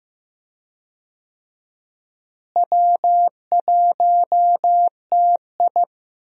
Morse code, heard as a keyed tone, about 15 wpm, 700 Hz. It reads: W1TI
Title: USA Call Signs - 4 Characters - Repeated 2x - 15wpm